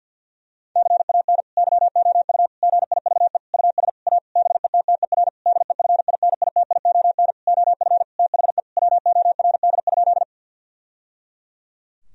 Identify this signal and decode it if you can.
Morse code, heard as a keyed tone, about 50 wpm, 700 Hz. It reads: CAN YOU GIVE US A BETTER DEFINITION OF THE WORD?
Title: Sentences from Top 1000 Words - Review Part 4 - 50wpm